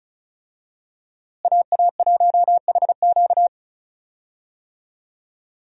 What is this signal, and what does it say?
Morse code, heard as a keyed tone, about 34 wpm, 700 Hz. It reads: AA1HQ
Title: USA Call Signs - 5 Characters - Repeated 2x - 35wpm